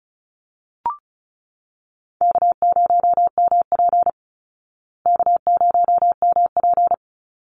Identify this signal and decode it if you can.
Morse code, heard as a keyed tone, about 34 wpm, 700 Hz. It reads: K0MP K0MP
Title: Easy Call Signs Repeated 3x - 35wpm